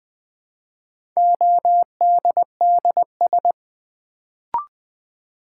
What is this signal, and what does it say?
Morse code, heard as a keyed tone, about 20 wpm, 700 Hz. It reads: ODDS
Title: Build A Word - 2 Letters At A Time - 4-Letter Words - 20wpm